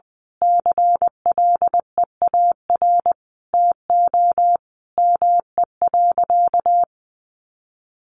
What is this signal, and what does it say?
Morse code, heard as a keyed tone, about 20 wpm, 700 Hz. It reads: CLEAR TO ME.
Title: Sentences from Top 1300 Words 20wpm